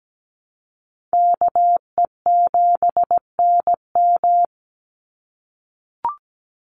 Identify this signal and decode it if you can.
Morse code, heard as a keyed tone, about 17 wpm, 700 Hz. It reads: KE7NM
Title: USA Call Signs 17wpm